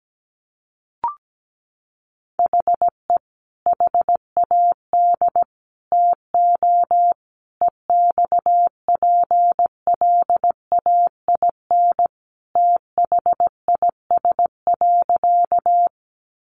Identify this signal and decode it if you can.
Morse code, heard as a keyed tone, about 17 wpm, 700 Hz. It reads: HE HAD TO EXPLAIN THIS.
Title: Sentences from Top 1100 Words 17wpm